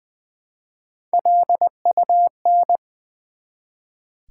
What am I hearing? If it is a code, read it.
Morse code, heard as a keyed tone, about 20 wpm, 700 Hz. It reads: LUN
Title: Build A Word - 1 Letter At A Time - 5-Letter Words - 20wpm